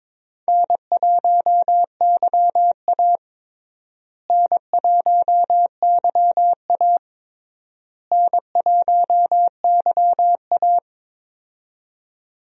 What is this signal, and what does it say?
Morse code, heard as a keyed tone, about 22 wpm, 700 Hz. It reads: N1YA N1YA N1YA
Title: USA Call Signs - 4 Characters - Repeated 3x - 22wpm